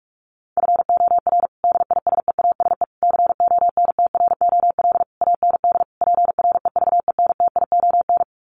Morse code, heard as a keyed tone, about 44 wpm, 700 Hz. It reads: FOR DISEASE CONTROL AND PREVENTION
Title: Sets of 5 Words Encore2 45wpm